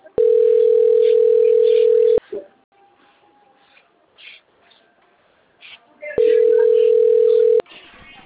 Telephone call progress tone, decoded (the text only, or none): ringback tone